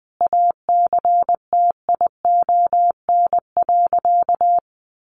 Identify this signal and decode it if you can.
Morse code, heard as a keyed tone, about 20 wpm, 700 Hz. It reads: ACTION.
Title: Sentences from Top 1300 Words 20wpm